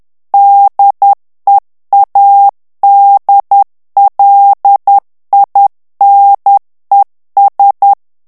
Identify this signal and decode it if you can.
Morse code, heard as a keyed tone, about 10 wpm, 800 Hz. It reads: DEADLINES
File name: ANGLAIS texte clair 3 vitesse 10 mots minute